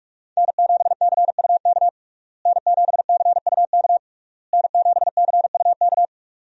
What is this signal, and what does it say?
Morse code, heard as a keyed tone, about 46 wpm, 700 Hz. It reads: N7CUK N7CUK N7CUK
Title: USA Call Signs - 5 Characters - Repeated 3x - 45wpm